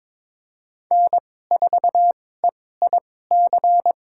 Morse code, heard as a keyed tone, about 22 wpm, spaced as for 16 wpm, 700 Hz. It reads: N4EIC
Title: USA Call Signs - 5 Characters - 2x Character Spacing - 22wpm